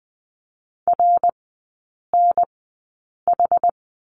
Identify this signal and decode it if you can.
Morse code, heard as a keyed tone, about 20 wpm, spaced as for 8 wpm, 700 Hz. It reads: RNH
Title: 060 Lesson 11e - Letter h - Three Characters 20wpm